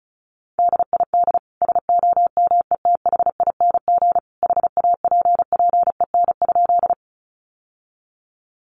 Morse code, heard as a keyed tone, about 34 wpm, 700 Hz. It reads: DID SOMETHING HAPPEN?
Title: Sentences from Top 600 Words 35wpm